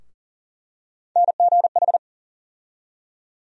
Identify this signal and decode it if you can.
Morse code, heard as a keyed tone, about 40 wpm, 700 Hz. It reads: NGH
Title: USA Call Sign Postfixes - 3 Letters 40wpm